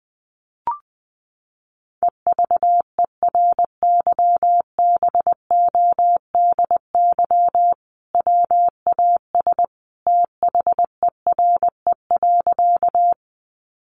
Morse code, heard as a keyed tone, about 20 wpm, 700 Hz. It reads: EVERYBODY WAS THERE.
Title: Sentences from Top 1400 Words 20wpm